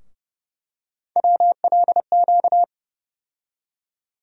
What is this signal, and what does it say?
Morse code, heard as a keyed tone, about 30 wpm, 700 Hz. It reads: WLQ
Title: USA Call Sign Postfixes - 3 Letters 30wpm